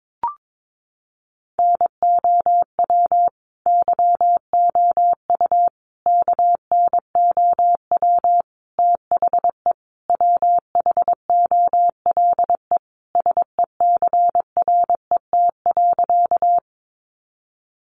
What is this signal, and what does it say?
Morse code, heard as a keyed tone, about 22 wpm, 700 Hz. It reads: NOW YOU KNOW THE WHOLE SECRET.
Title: Sentences from Top 1200 Words 22wpm